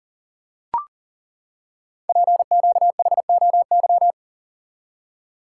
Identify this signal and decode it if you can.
Morse code, heard as a keyed tone, about 38 wpm, 690 Hz. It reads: PQHOY
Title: Code Groups - 5 Letters - 40wpm